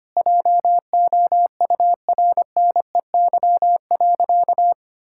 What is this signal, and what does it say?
Morse code, heard as a keyed tone, about 25 wpm, 700 Hz. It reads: JOURNEY.
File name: Sentences with 2-Word Phrases - 25wpm